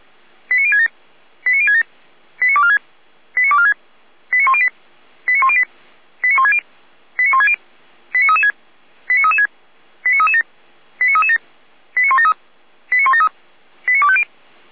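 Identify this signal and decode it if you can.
Five-tone selective call (ZVEI), heard as quick runs of five tones, tones of about 70 ms. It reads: 89067 89067 89246 89246 89108 89108 89160 89160 89336 89336 89337 89337 89173 89173 89250
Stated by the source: ZVEI five-tone call received on SDR.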